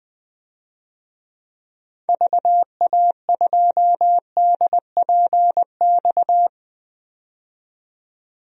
Morse code, heard as a keyed tone, about 21 wpm, 700 Hz. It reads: VA2DPX
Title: DXCC-100 Call Signs - 6 Characters - 20wpm